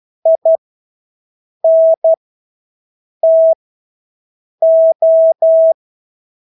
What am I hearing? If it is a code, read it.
Morse code, heard as a keyed tone, about 12 wpm, spaced as for 6 wpm, 650 Hz. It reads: INTO
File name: Engelske ord 12-6 wpm